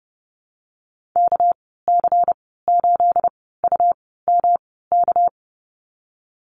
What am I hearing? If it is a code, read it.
Morse code, heard as a keyed tone, about 30 wpm, spaced as for 17 wpm, 700 Hz. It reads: KC8UMK
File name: Lesson 37g - Letter x - Call Signs 30wpm